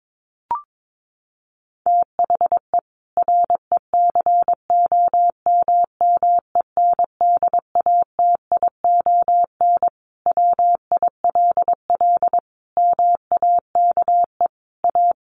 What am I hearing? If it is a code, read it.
Morse code, heard as a keyed tone, about 22 wpm, 700 Hz. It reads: THE RECOMMENDATION WILL MAKE A
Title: Sentences from Top 1400 Words 22wpm